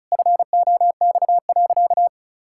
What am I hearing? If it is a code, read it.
Morse code, heard as a keyed tone, about 34 wpm, 700 Hz. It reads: FOX.
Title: Pangram Challenge v2 - 35wpm